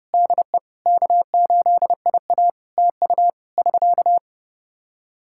Morse code, heard as a keyed tone, about 30 wpm, 700 Hz. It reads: DE K8IA TU <SK>
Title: K1USN SST QSO Practice 30wpm